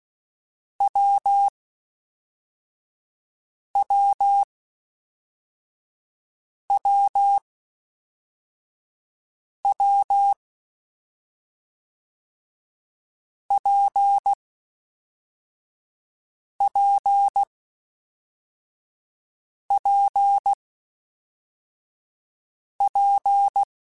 Morse code, heard as a keyed tone, about 16 wpm, 800 Hz. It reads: W W W W P P P P
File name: Difficult Combos / Difficult Combos-mp3s / WPJ WPJ1 @10